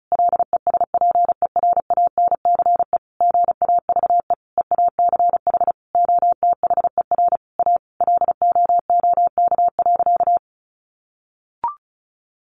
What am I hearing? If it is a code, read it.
Morse code, heard as a keyed tone, about 34 wpm, 700 Hz. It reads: LESPERANCE GAVE EACH OTHER A LOOK.
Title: A Sound Of Thunder 35wpm